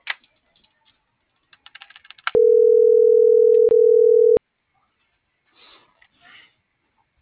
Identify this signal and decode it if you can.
Ringback tone (North America), 440 + 480 Hz.
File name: CENTRAL MULTIMODAL / out francisco.cobos